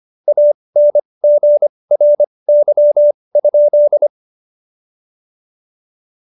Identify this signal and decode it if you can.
Morse code, heard as a keyed tone, about 25 wpm, spaced as for 20 wpm, 590 Hz. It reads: ANGRY?
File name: Jokes-25-char-20-eff